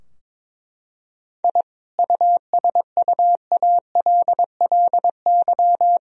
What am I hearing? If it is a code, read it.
Morse code, heard as a keyed tone, about 22 wpm, 700 Hz. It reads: I USUALLY
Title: Sentences from Top 1000 Words - Review Part 3 - 22wpm